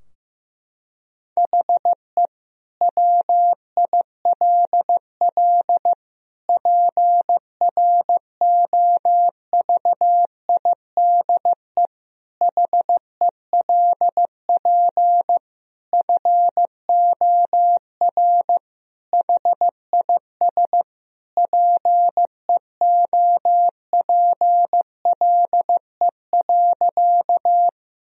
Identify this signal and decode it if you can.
Morse code, heard as a keyed tone, about 15 wpm, 700 Hz. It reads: HE WILL PROVIDE HELP FOR HIS PEOPLE.
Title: Sentences from Top 1000 Words - Review Part 3 - 15wpm